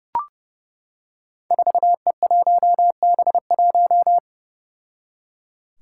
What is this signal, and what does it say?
Morse code, heard as a keyed tone, about 28 wpm, 700 Hz. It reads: 4E1B1
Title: Five Challenging Letter-Number Combinations 30wpm